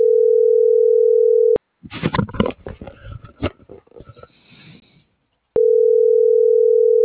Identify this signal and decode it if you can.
Ringback tone (North America), 440 + 480 Hz.
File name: CENTRAL MAERSK / out sandovalmsk